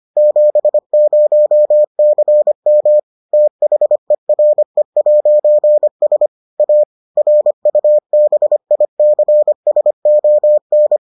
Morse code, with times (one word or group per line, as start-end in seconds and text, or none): 0.2-3.0 70CM
3.3-6.3 THERE'S
6.6-6.8 A
7.2-11.0 RUBICSON